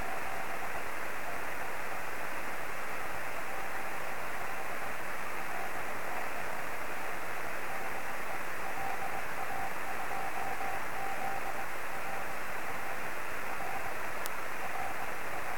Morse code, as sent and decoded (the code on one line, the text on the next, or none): none